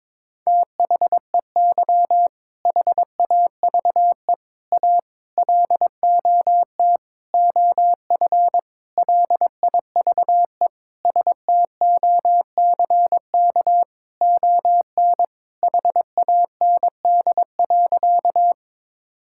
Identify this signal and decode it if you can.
Morse code, heard as a keyed tone, about 22 wpm, 700 Hz. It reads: THEY HAVE A LOT OF LIVE STOCK ON HAND.
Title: Sentences from Top 1000 Words - Review Part 5 - 22wpm